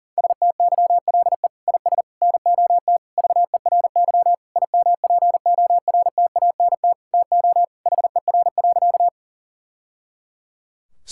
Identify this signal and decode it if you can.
Morse code, heard as a keyed tone, about 40 wpm, 700 Hz. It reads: STYLE IS NOT VERY IMPORTANT TO HER.
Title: Sentences from Top 1000 Words - Review Part 5 - 40wpm